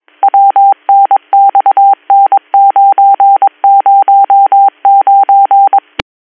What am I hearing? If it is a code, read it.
Morse code, heard as a keyed tone, about 22 wpm, 790 Hz. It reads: WNXN909
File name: 202- .0-call 30530